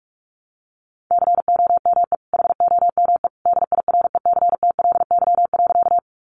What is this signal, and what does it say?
Morse code, heard as a keyed tone, about 44 wpm, 700 Hz. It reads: COME HOME DIRECTLY.
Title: Sentences from Top 800 Words 45wpm